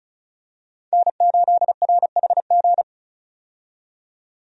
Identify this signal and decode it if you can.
Morse code, heard as a keyed tone, about 36 wpm, 700 Hz. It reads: N8RHG
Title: USA Call Signs - 5 Characters - 35wpm